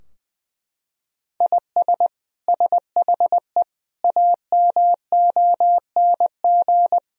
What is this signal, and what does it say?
Morse code, heard as a keyed tone, about 20 wpm, 700 Hz. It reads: IS SHE AMONG
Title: Sentences from Top 1000 Words - Review Part 3 - 20wpm